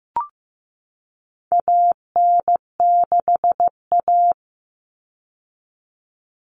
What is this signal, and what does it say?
Morse code, heard as a keyed tone, about 15 wpm, 700 Hz. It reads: AN6A
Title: Common Cut Numbers - 4-Digit 15wpm